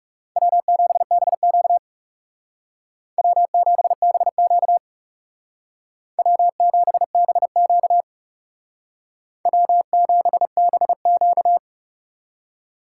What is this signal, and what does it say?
Morse code, heard as a keyed tone, about 38 wpm, 700 Hz. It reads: W7BQ W7BQ W7BQ W7BQ